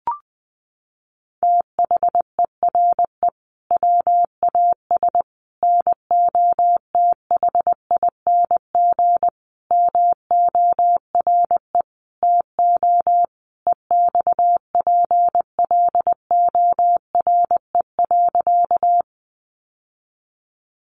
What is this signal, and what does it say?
Morse code, heard as a keyed tone, about 20 wpm, 700 Hz. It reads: THERE WAS NOTHING MORE TO EXPLORE.
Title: Sentences from Top 1100 Words 20wpm